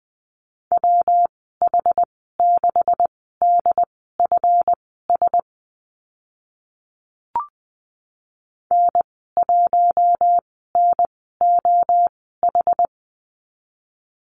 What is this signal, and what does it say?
Morse code, heard as a keyed tone, about 20 wpm, spaced as for 14 wpm, 700 Hz. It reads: WH6DFS N1NOH
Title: 147 Lesson 24g - Letter y - Call Signs 20wpm